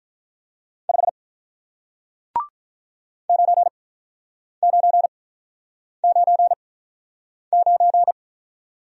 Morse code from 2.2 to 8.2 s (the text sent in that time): 9999